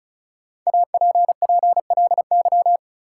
Morse code, heard as a keyed tone, about 34 wpm, 700 Hz. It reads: APPLY
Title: Sentences from Top 1200 Words - New Word Drill - 35wpm